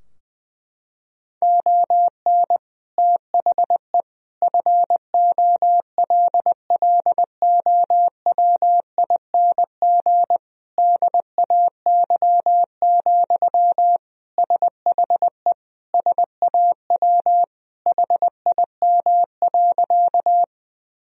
Morse code, heard as a keyed tone, about 20 wpm, 700 Hz. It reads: ON THE FOLLOWING DAY, SHE SAW HIM.
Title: Sentences from Top 1000 Words - Review Part 4 - 20wpm